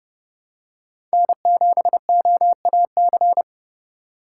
Morse code, heard as a keyed tone, about 30 wpm, 700 Hz. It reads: N7OAC
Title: USA Call Signs - 5 Characters - 2x Character Spacing - 30wpm